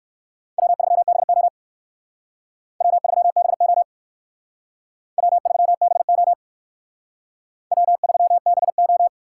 Morse code, heard as a keyed tone, about 57 wpm, 700 Hz. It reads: W3BO W3BO W3BO W3BO